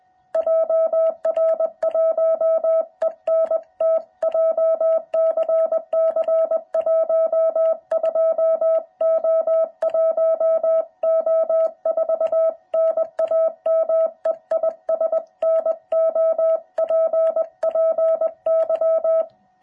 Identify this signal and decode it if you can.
Morse code, heard as a keyed tone, about 20 wpm, 660 Hz. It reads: JR1ENT JCC 12O1O4 NAME IS NOPPY